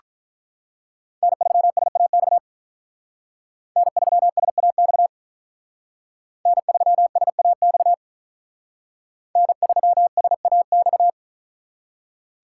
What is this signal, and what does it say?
Morse code, heard as a keyed tone, about 45 wpm, 700 Hz. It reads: N3SAX N3SAX N3SAX N3SAX